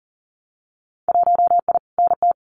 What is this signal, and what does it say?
Morse code, heard as a keyed tone, about 40 wpm, 700 Hz. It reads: 1I NT
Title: Winter Field Day Exchanges Encore 40wpm